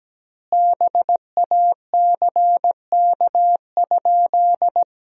Morse code, heard as a keyed tone, about 17 wpm, 700 Hz. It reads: BACK?
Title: 3x Word-Spacing - Sentences from Top 400 Words - 17wpm